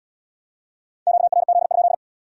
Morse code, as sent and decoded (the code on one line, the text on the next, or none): --.- .-. --.. ..--..
QRZ?